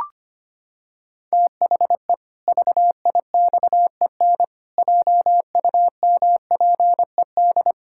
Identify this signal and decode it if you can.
Morse code, heard as a keyed tone, about 25 wpm, 700 Hz. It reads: THE VIXEN JUMPED
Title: Pangram Challenge v2 - 25wpm